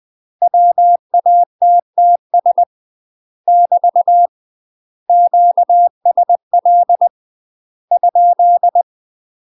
Morse code, heard as a keyed tone, about 20 wpm, 700 Hz. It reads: WATTS <BT> QSL ?
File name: UK MASTER QSOs TRAINING / 20WPM / UK MASTER QSOs 0010